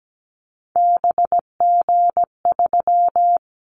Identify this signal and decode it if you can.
Morse code, heard as a keyed tone, about 17 wpm, 700 Hz. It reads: BG3
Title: Callsign Prefixes for Top 100 DXCC - 17wpm